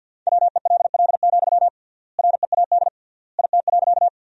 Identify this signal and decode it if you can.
Morse code, heard as a keyed tone, about 50 wpm, 700 Hz. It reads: WELL, READ IT.
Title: 3x Word-Spacing - Sentences from Top 200 Words - 50wpm